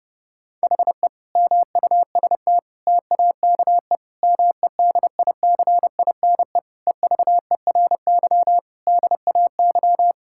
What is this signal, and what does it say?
Morse code, heard as a keyed tone, about 30 wpm, 700 Hz. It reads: HE MUST TAKE MEDICINE EVERY DAY
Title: Sentences from Top 1000 Words - Review Part 6 - 30wpm